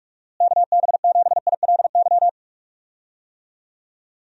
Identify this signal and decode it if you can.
Morse code, heard as a keyed tone, about 46 wpm, 700 Hz. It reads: KB7ILY
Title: USA Call Signs - 6 Characters - Repeated 3x - 45wpm